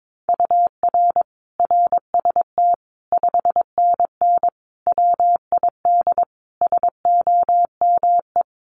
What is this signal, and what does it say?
Morse code, heard as a keyed tone, about 22 wpm, 700 Hz. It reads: UR RST 5NN WID SOME
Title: Common QSO Phrases 22wpm